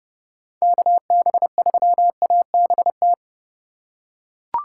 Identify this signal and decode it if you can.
Morse code, heard as a keyed tone, about 30 wpm, 700 Hz. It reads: KB3ABT
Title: USA Call Signs - 6 Characters - 3x Character Spacing - 30wpm